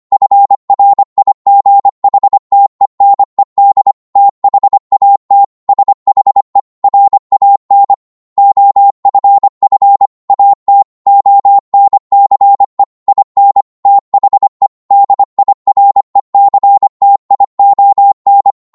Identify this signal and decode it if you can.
Morse code, heard as a keyed tone, about 25 wpm, 800 Hz. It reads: FRIGHTENED THAT SHE RAN OFF AT ONCE IN THE DIRECTION
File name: 25wpm